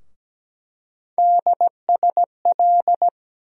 Morse code, heard as a keyed tone, about 17 wpm, 700 Hz. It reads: DSL
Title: USA Call Sign Postfixes - 3 Letters 17wpm